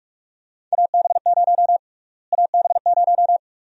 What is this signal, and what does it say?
Morse code, heard as a keyed tone, about 42 wpm, 700 Hz. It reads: AB0 AB0
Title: USA Call Sign Prefixes - Repeated 2x - 45wpm